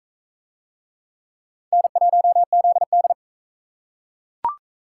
Morse code, heard as a keyed tone, about 37 wpm, 700 Hz. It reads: N1ZD